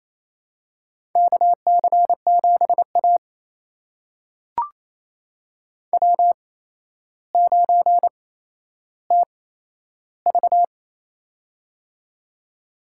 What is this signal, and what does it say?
Morse code, heard as a keyed tone, about 29 wpm, spaced as for 12 wpm, 700 Hz. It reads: KC7A W9TV
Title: USA Call Signs - 4 Characters - 8x Character Spacing - 28wpm